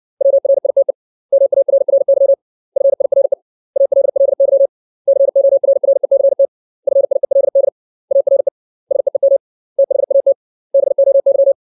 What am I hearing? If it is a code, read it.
Morse code, heard as a keyed tone, about 60 wpm, 550 Hz. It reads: WRITE MARRY FIRE ALLY CORRECT FIELD ARE SEEM THAT BOY
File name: 632 words / 632 Words 60WPM